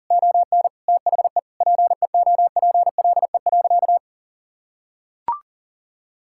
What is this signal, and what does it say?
Morse code, heard as a keyed tone, about 40 wpm, 700 Hz. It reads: ON THE PEOPLE.
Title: Sentences from Top 1000 Words - Review Part 5 - 40wpm